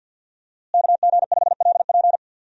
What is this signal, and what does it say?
Morse code, heard as a keyed tone, about 51 wpm, 700 Hz. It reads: KG5LP
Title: USA Call Signs - 5 Characters - Repeated 3x - 50wpm